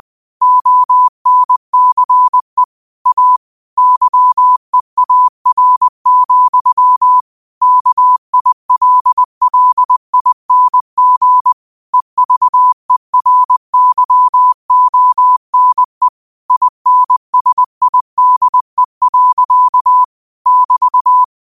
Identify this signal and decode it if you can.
Morse code, heard as a keyed tone, about 20 wpm, 1000 Hz. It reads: ONCE A YEAR, KILLING EVERYONE INSIDE. <BT>
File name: qotdfiles / 20wpm